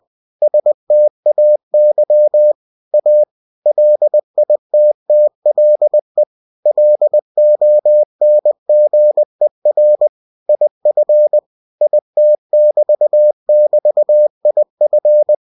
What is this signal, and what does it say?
Morse code, heard as a keyed tone, about 20 wpm, 600 Hz. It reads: STAY A LITTLE LONGER IF IT<BT><BT>IF
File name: cw-books / winnie-the-pooh / winnie-20wpm-page-0015